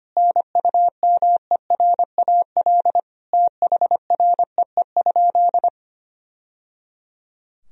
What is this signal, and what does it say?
Morse code, heard as a keyed tone, about 25 wpm, 700 Hz. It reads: NUMERAL THREE?
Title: Sentences from Top 1000 Words - Review Part 2 - 25wpm